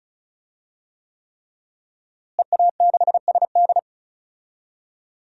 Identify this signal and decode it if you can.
Morse code, heard as a keyed tone, about 36 wpm, 700 Hz. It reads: EA6SD
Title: DXCC-100 Call Signs - 5 Characters - Repeated 3x - 35wpm